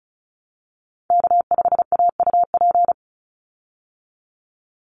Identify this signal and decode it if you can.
Morse code, heard as a keyed tone, about 34 wpm, 700 Hz. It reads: K5AUP
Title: Call Signs Repeated 2x - 35wpm